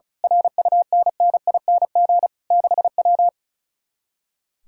Morse code, heard as a keyed tone, about 33 wpm, 700 Hz. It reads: RUNNING 6W
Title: Speed-Racing Common QSO Phrases Encore - 35wpm